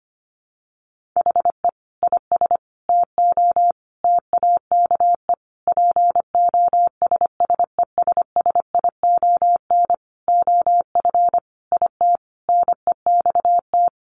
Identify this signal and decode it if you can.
Morse code, heard as a keyed tone, about 25 wpm, 700 Hz. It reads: HE IS TO TAKE POSSESSION OF IT NEXT
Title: Sentences from Top 1400 Words 25wpm